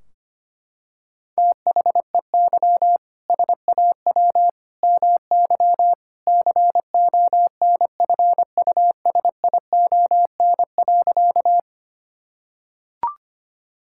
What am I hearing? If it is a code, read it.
Morse code, heard as a keyed tone, about 25 wpm, 700 Hz. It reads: THEY SAW MY CONFUSION.
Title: Sentences from Top 1600 Words 25wpm